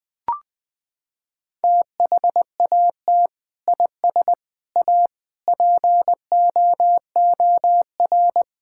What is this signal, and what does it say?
Morse code, heard as a keyed tone, about 21 wpm, 700 Hz. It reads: THAT IS A POOR
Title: Sentences from Top 1000 Words - Review Part 6 - 20wpm